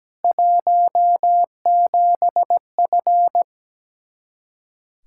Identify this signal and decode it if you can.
Morse code, heard as a keyed tone, about 17 wpm, 700 Hz. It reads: 17F
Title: K1USN SST QSO Practice 17wpm